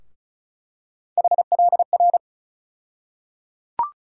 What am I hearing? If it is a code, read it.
Morse code, heard as a keyed tone, about 34 wpm, 700 Hz. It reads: HLR
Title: USA Call Sign Postfixes - 3 Letters 35wpm